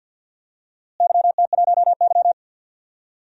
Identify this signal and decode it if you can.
Morse code, heard as a keyed tone, about 52 wpm, 700 Hz. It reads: YT1Y
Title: DXCC-100 Call Signs - 4 Characters - Repeated 2x - 50wpm